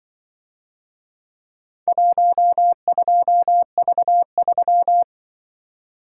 Morse code, heard as a keyed tone, about 23 wpm, 700 Hz. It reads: 12V3
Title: Speed-Racing - Four Letter-Number Combinations - 20wpm